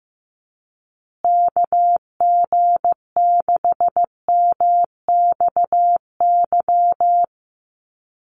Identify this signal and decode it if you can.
Morse code, heard as a keyed tone, about 15 wpm, 700 Hz. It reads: KG6MXY
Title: USA Call Signs Repeated 3x 15wpm